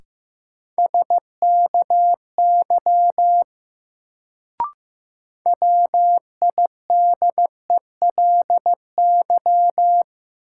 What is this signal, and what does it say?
Morse code, heard as a keyed tone, about 15 wpm, 700 Hz. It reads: SKY WIDELY
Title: Sentences from Top 1200 Words - New Word Drill Alt - 15wpm